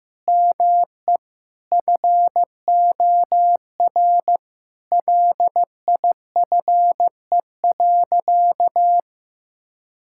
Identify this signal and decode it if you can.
Morse code, heard as a keyed tone, about 15 wpm, 700 Hz. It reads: ME FOR LIFE.
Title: Sentences from Top 1000 Words - Review Part 6 - 15wpm